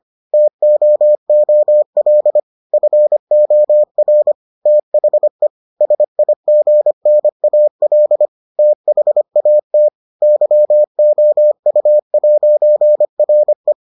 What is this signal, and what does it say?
Morse code, heard as a keyed tone, about 25 wpm, 600 Hz. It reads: TOOL FOR THE SIGNAL THAT YOU'RE